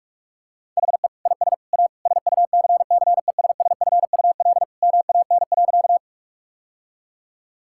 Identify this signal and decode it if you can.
Morse code, heard as a keyed tone, about 42 wpm, 700 Hz. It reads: HE IS A SUCCESSFUL MAN.
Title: Sentences from Top 1000 Words - Review Part 6 - 45wpm